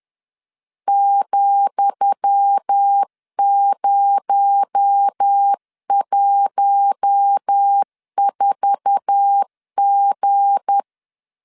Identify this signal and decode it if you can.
Morse code, heard as a keyed tone, about 10 wpm, 800 Hz. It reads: ,014G
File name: LEZIONI / 10 SIGNS / 10D COMA 10wpm 3013